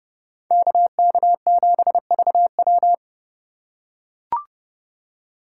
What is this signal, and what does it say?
Morse code, heard as a keyed tone, about 29 wpm, 700 Hz. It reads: KK7VW